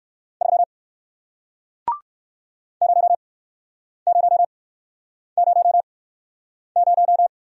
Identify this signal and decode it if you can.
Morse code, heard as a keyed tone, about 61 wpm, spaced as for 9 wpm, 700 Hz. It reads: 3 0000